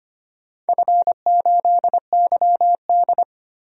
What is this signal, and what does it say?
Morse code, heard as a keyed tone, about 25 wpm, 700 Hz. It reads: F8YD
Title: DXCC-100 Call Signs - 4 Characters - 25wpm